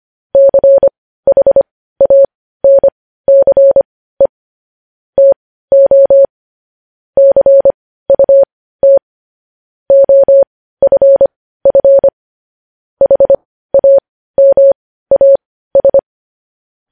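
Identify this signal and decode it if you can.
Morse code, heard as a keyed tone, about 25 wpm, spaced as for 15 wpm, 550 Hz. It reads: CHANCE TO CUT OFF HAMAS
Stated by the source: Morse code news headlines broadcast.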